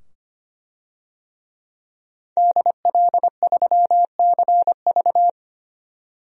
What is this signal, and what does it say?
Morse code, heard as a keyed tone, about 25 wpm, 700 Hz. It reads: DL3CV
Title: DXCC-100 Call Signs - 5 Characters - 25wpm